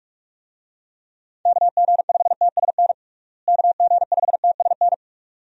Morse code, heard as a keyed tone, about 46 wpm, 700 Hz. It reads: KG5TSN KG5TSN
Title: USA Call Signs - 6 Characters - Repeated 2x - 45wpm